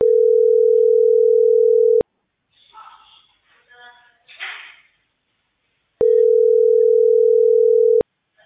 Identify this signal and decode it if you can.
Ringback tone (North America), 440 + 480 Hz.